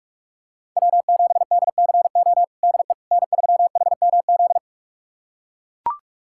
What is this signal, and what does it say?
Morse code, heard as a keyed tone, about 41 wpm, 700 Hz. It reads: W7DCO DE N3HMZ
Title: USA Callsign DE Callsign - 5 Characters - 45wpm